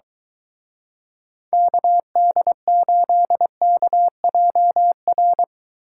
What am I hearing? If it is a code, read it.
Morse code, heard as a keyed tone, about 23 wpm, 700 Hz. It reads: KD8KJR